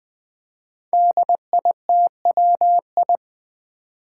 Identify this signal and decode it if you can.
Morse code, heard as a keyed tone, about 20 wpm, 700 Hz. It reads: DITWI
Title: Five Challenging Letter-Number Combinations 20wpm